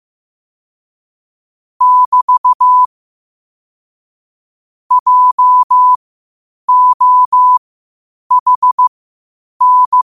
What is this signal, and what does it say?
Morse code, heard as a keyed tone, about 15 wpm, spaced as for 8 wpm, 1000 Hz. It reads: <BT> JOHN
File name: qotdfiles / 8wpm